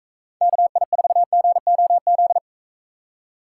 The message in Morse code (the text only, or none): KI4GOZ